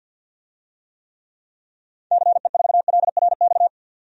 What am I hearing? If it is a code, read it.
Morse code, heard as a keyed tone, about 50 wpm, 700 Hz. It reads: CE4LRX
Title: DXCC-100 Call Signs - 6 Characters - 50wpm